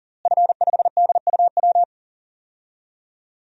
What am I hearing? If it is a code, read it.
Morse code, heard as a keyed tone, about 37 wpm, 700 Hz. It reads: F5DUW
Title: DXCC-100 Call Signs - 5 Characters - Repeated 2x - 40wpm